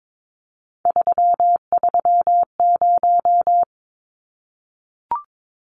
Morse code, heard as a keyed tone, about 22 wpm, 700 Hz. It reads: 330
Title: Numbers - 3-Digit 22wpm